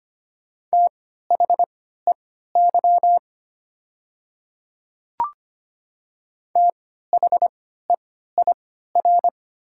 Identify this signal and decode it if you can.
Morse code, heard as a keyed tone, about 25 wpm, spaced as for 14 wpm, 690 Hz. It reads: THEY THEIR
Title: 3x Character Spacing - Top 100 Words - 25wpm